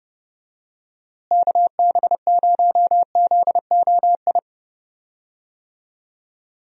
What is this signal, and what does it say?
Morse code, heard as a keyed tone, about 29 wpm, 700 Hz. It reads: KB0ZOI
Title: USA Call Signs - 6 Characters - Repeated 2x - 30wpm